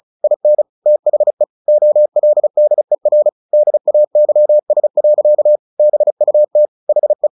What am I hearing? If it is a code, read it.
Morse code, heard as a keyed tone, about 35 wpm, 600 Hz. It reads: IN THE OLDER DAYS. BUT HE
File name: the old man and the sea / the old man and the sea 1 35wpm